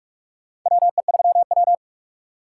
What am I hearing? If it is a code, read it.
Morse code, heard as a keyed tone, about 46 wpm, 700 Hz. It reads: WE3W
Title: USA Call Signs - 2x Character Spacing - 45wpm